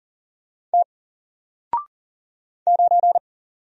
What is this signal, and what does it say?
Morse code, heard as a keyed tone, about 40 wpm, 700 Hz. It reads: T 9
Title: Single Letter-Number Alt 40wpm